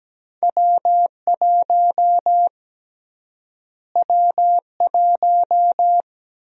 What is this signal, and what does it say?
Morse code, heard as a keyed tone, about 17 wpm, 700 Hz. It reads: W1 W1
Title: USA Call Sign Prefixes - Repeated 2x - 17wpm